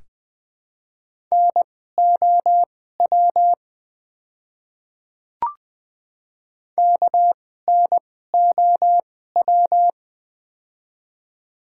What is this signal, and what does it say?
Morse code, heard as a keyed tone, about 20 wpm, spaced as for 15 wpm, 700 Hz. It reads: NOW KNOW
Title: 2x Character Spacing - Top 100 Words - 20wpm